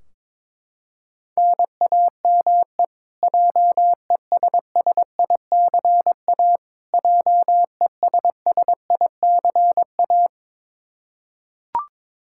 Morse code, heard as a keyed tone, about 22 wpm, 700 Hz. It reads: NAME JESSICA JESSICA
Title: Practice QSO - Exchange 1 - Beginning - 22wpm